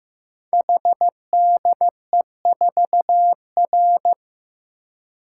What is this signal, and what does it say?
Morse code, heard as a keyed tone, about 15 wpm, 700 Hz. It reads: HDE4R
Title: Code Groups - 5 Letter:Numbers - 15wpm